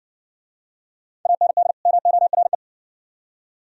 Morse code, heard as a keyed tone, about 60 wpm, 700 Hz. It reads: AND MORE